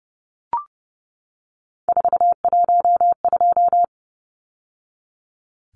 Morse code, heard as a keyed tone, about 30 wpm, 700 Hz. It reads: 412
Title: Numbers - 3-Digit 30wpm